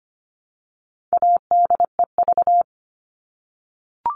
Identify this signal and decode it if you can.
Morse code, heard as a keyed tone, about 25 wpm, 700 Hz. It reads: ADEV
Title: All Cut Numbers - 4-Digit 25wpm